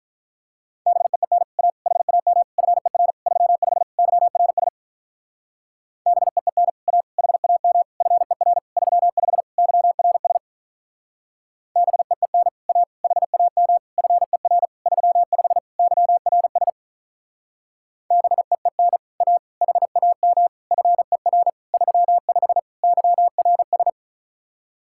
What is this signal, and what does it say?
Morse code, heard as a keyed tone, about 45 wpm, 700 Hz. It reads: BEEN A HAM FER 35 YRS BEEN A HAM FER 35 YRS BEEN A HAM FER 35 YRS BEEN A HAM FER 35 YRS